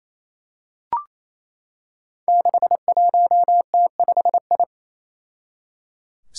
Morse code, heard as a keyed tone, about 28 wpm, 700 Hz. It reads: 61T5I
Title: Five Challenging Letter-Number Combinations 28wpm